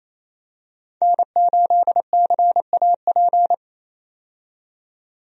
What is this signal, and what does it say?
Morse code, heard as a keyed tone, about 28 wpm, 700 Hz. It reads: N8CAP
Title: USA Call Signs - 5 Characters - 28wpm